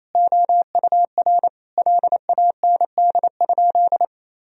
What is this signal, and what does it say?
Morse code, heard as a keyed tone, about 28 wpm, 700 Hz. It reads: OUR LAND?
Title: Sentences from Top 1000 Words - Review Part 4 - 28wpm